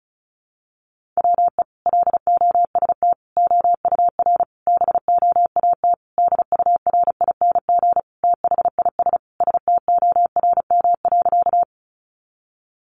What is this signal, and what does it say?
Morse code, heard as a keyed tone, about 34 wpm, 700 Hz. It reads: WE LOST OUR BOAT DURING THIS STORM.
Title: Sentences from Top 1500 Words 35wpm